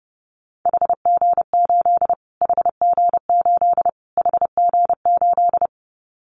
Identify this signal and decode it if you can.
Morse code, heard as a keyed tone, about 30 wpm, 700 Hz. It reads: HG8 HG8 HG8
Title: Callsign Prefixes for Top 25 DXCC - Repeated 3x - 30wpm